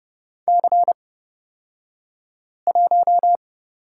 Morse code, heard as a keyed tone, about 29 wpm, spaced as for 5 wpm, 700 Hz. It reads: C1
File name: Lesson 38d - Letter q - Two Characters 30wpm